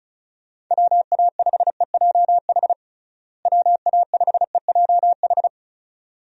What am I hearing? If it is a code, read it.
Morse code, heard as a keyed tone, about 35 wpm, 700 Hz. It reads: WA5EJH WA5EJH
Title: USA Call Signs - 6 Characters - Repeated 2x - 35wpm